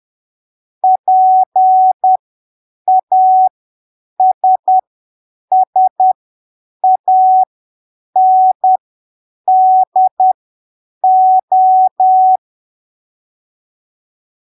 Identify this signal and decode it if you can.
Morse code, heard as a keyed tone, about 10 wpm, spaced as for 7 wpm, 750 Hz. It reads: PASSANDO
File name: arquivos / texto / texto 26 10ppm